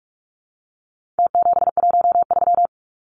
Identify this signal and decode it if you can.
Morse code, heard as a keyed tone, about 44 wpm, 700 Hz. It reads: T713
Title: Numbers - Contest Style - 4-Digit 45wpm